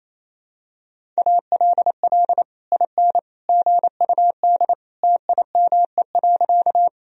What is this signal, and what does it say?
Morse code, heard as a keyed tone, about 28 wpm, 700 Hz. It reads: ALL IN GUD TIME.
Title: Sentences from Top 200 Words and Abbreviations 28wpm